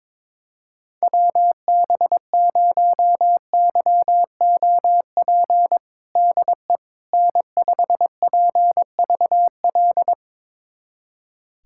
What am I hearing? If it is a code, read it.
Morse code, heard as a keyed tone, about 22 wpm, 700 Hz. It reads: WB0YOP DE N5PVL
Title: USA Callsign DE Callsign 22wpm